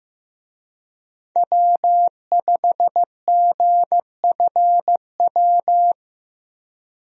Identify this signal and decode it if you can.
Morse code, heard as a keyed tone, about 15 wpm, 700 Hz. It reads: W5GFW
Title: USA Call Signs - 5 Characters - 15wpm